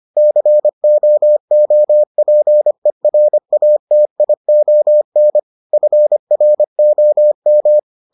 Morse code, heard as a keyed tone, about 25 wpm, 600 Hz. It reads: COOPERATION FROM